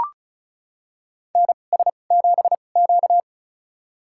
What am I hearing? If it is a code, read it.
Morse code, heard as a keyed tone, about 34 wpm, spaced as for 25 wpm, 700 Hz. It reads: NS7Q
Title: USA Call Signs - 4 Characters - 2x Character Spacing - 35wpm